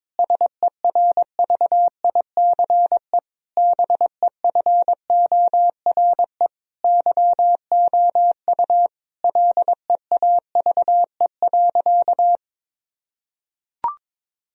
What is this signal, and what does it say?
Morse code, heard as a keyed tone, about 22 wpm, 700 Hz. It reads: SERVICE BEFORE YOU LEAVE.
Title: Sentences from Top 1000 Words - Review Part 4 - 22wpm